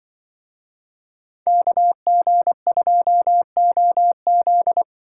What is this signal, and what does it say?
Morse code, heard as a keyed tone, about 24 wpm, 700 Hz. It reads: KG2OZ
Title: Speed Racing - USA 2x2 Call Signs 20wpm